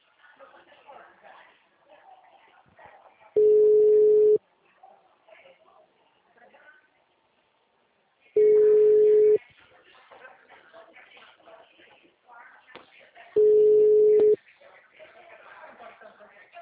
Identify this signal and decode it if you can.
Ringback tone (Europe, 425 Hz), with speech.